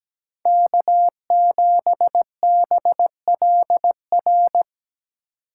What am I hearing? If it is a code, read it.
Morse code, heard as a keyed tone, about 17 wpm, 700 Hz. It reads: K7BLR
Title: USA Call Signs - 5 Characters - 6x Character Spacing - 17wpm